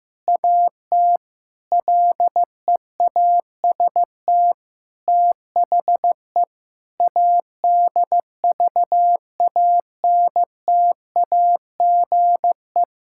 Morse code, heard as a keyed tone, about 16 wpm, 700 Hz. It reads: AT LEAST THE ADVANTAGE
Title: Sentences from Top 1000 Words - Review Part 6 - 15wpm